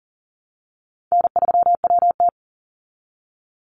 Morse code, heard as a keyed tone, about 40 wpm, 700 Hz. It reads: N3WT
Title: Call Signs Repeated 3x - 40wpm